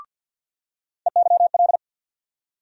Morse code, heard as a keyed tone, about 53 wpm, 700 Hz. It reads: EYL
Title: USA Call Sign Postfixes - 3 Letters 50wpm